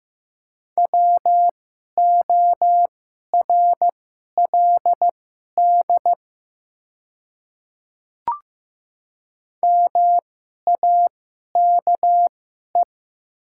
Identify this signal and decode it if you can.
Morse code, heard as a keyed tone, about 15 wpm, spaced as for 11 wpm, 700 Hz. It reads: WORLD MAKE
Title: 2x Character Spacing - Top 200 Words - 15wpm